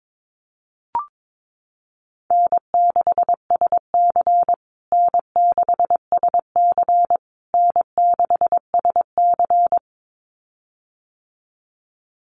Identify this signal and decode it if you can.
Morse code, heard as a keyed tone, about 22 wpm, 700 Hz. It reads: N6SC N6SC N6SC
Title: USA Call Signs Repeated 3x 22wpm